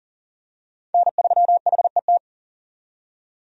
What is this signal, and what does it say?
Morse code, heard as a keyed tone, about 40 wpm, 700 Hz. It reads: N3HET
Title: USA Call Signs - 5 Characters - 40wpm